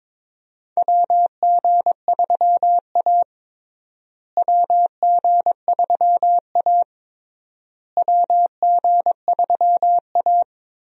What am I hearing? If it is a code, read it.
Morse code, heard as a keyed tone, about 22 wpm, 700 Hz. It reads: WG3A WG3A WG3A
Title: USA Call Signs - 4 Characters - Repeated 3x - 22wpm